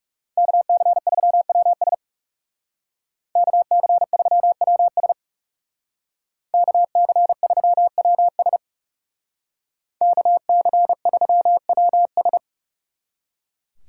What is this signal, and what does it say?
Morse code, heard as a keyed tone, about 37 wpm, 700 Hz. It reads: KC3WS KC3WS KC3WS KC3WS
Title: Speed Racing - USA 2x2 Call Signs 30wpm